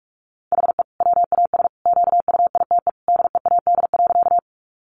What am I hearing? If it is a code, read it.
Morse code, heard as a keyed tone, about 44 wpm, 700 Hz. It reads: HE WAS QUITE DEAD.
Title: Sentences from Top 1500 Words 45wpm